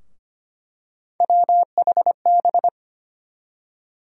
Morse code, heard as a keyed tone, about 25 wpm, 700 Hz. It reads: WHB
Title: USA Call Sign Postfixes - 3 Letters 25wpm